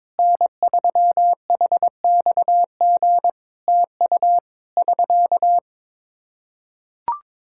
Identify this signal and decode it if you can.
Morse code, heard as a keyed tone, about 22 wpm, 700 Hz. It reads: N3HXG TU <SK>
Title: K1USN SST QSO Practice 22wpm